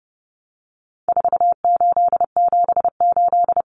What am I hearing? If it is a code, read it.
Morse code, heard as a keyed tone, about 30 wpm, 700 Hz. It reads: 4878
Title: Common Cut Numbers - 4-Digit 30wpm